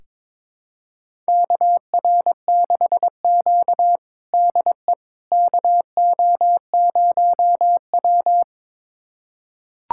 Morse code, heard as a keyed tone, about 22 wpm, 700 Hz. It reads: KR6Q DE KO0W
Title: USA Callsign DE Callsign - 4 Characters - 22wpm